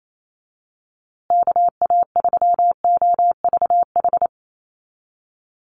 Morse code, heard as a keyed tone, about 28 wpm, 700 Hz. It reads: KA3OVH
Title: USA Call Signs Repeated 2x 28wpm